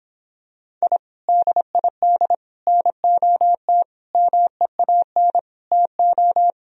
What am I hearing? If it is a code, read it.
Morse code, heard as a keyed tone, about 26 wpm, 700 Hz. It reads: I DID NOT MEAN TO